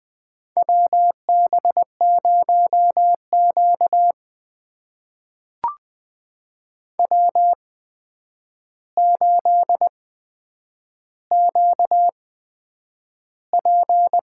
Morse code, heard as a keyed tone, about 20 wpm, spaced as for 9 wpm, 700 Hz. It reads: WB0Q W8QP
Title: USA Call Signs - 4 Characters - 8x Character Spacing - 20wpm